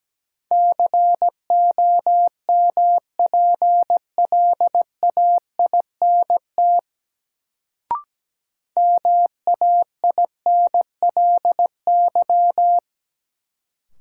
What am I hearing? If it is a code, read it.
Morse code, heard as a keyed tone, about 17 wpm, 700 Hz. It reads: COMPLAINT MAINLY
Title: Sentences from Top 1400 Words - New Word Drill - 17wpm